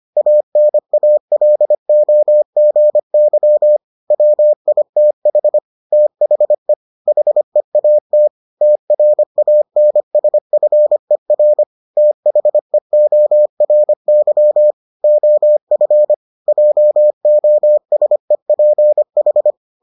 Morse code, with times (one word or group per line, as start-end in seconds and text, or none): 0.2-3.8 ANALOGY
4.1-5.6 WITH
5.9-6.7 THE
7.1-8.3 HEAT
8.6-11.6 TRANSFER
12.0-14.7 THEORY
15.0-16.1 OF
16.5-19.5 JOSEPH